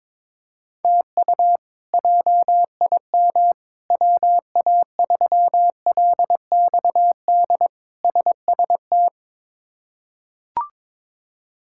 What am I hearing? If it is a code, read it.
Morse code, heard as a keyed tone, about 22 wpm, 700 Hz. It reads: TU JIM WA3LXD SST
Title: K1USN SST QSO Practice - 22wpm